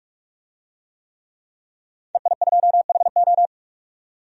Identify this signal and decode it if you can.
Morse code, heard as a keyed tone, about 46 wpm, 700 Hz. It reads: EI2HO
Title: DXCC-100 Call Signs - 5 Characters - 45wpm